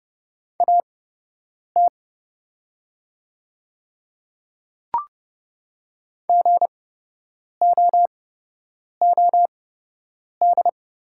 Morse code, heard as a keyed tone, about 30 wpm, spaced as for 8 wpm, 700 Hz. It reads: AT GOOD
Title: 8x Character Spacing - Top 300 Words - 30wpm